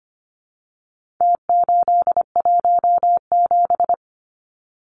Morse code, heard as a keyed tone, about 25 wpm, 700 Hz. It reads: T817
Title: Numbers - Contest Style - 4-Digit 25wpm